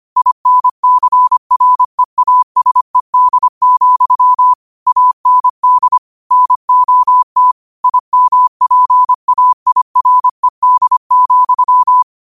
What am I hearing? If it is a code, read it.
Morse code, heard as a keyed tone, about 25 wpm, 1000 Hz. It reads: INCREASED, AND NOT IMPAIRED,